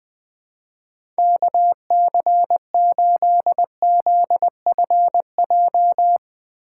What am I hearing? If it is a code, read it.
Morse code, heard as a keyed tone, about 20 wpm, 700 Hz. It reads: KC8ZFJ
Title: USA Call Signs - 6 Characters - 2x Character Spacing - 20wpm